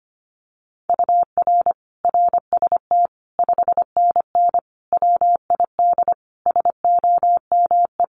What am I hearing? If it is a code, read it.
Morse code, heard as a keyed tone, about 25 wpm, 700 Hz. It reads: UR RST 5NN WID SOME
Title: Common QSO Phrases 25wpm